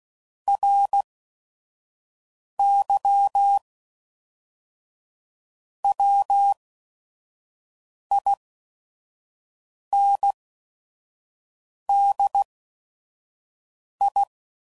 Morse code, heard as a keyed tone, about 16 wpm, spaced as for 5 wpm, 800 Hz. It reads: RY WINDI
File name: CW Code Groups @5WPM